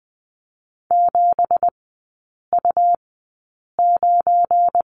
Morse code, heard as a keyed tone, about 20 wpm, spaced as for 8 wpm, 700 Hz. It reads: 7U9
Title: 172 Lesson 28e - Number 9 - Three Characters 20wpm